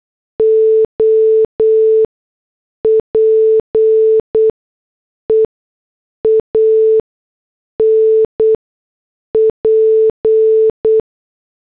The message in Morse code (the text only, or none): OPEAN P